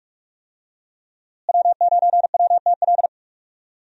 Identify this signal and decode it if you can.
Morse code, heard as a keyed tone, about 43 wpm, 700 Hz. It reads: W9WTL
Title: USA Call Signs - 5 Characters - Repeated 2x - 45wpm